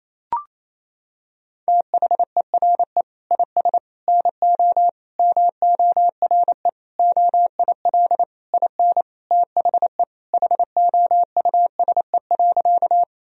Morse code, heard as a keyed tone, about 28 wpm, 700 Hz. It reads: THERE IS NO MORE OIL IN THE HOUSE.
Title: Sentences from Top 1000 Words - Review Part 4 - 28wpm